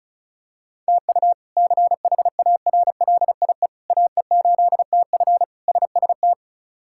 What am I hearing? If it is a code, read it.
Morse code, heard as a keyed tone, about 36 wpm, 700 Hz. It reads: TU CHARLIE AE8TF SST
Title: K1USN SST QSO Practice - 35wpm